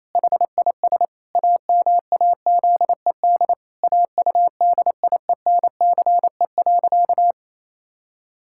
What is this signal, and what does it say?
Morse code, heard as a keyed tone, about 27 wpm, 700 Hz. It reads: HIS AMAZED AUDIENCE.
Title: Pangram Challenge v2 - 28wpm